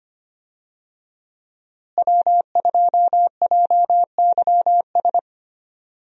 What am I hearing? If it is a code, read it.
Morse code, heard as a keyed tone, about 25 wpm, 700 Hz. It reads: W2JYS
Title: USA Call Signs - 5 Characters - Repeated 2x - 25wpm